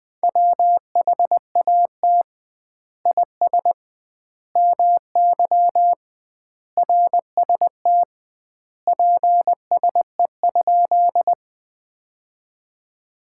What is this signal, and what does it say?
Morse code, heard as a keyed tone, about 20 wpm, 700 Hz. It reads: WHAT IS MY RST PSE?
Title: Common QSO Phrases - 2x Word-Spacing - 20wpm